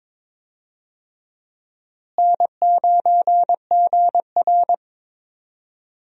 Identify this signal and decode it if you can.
Morse code, heard as a keyed tone, about 21 wpm, 700 Hz. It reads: N9GR
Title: USA Call Signs - 4 Characters - Repeated 3x - 22wpm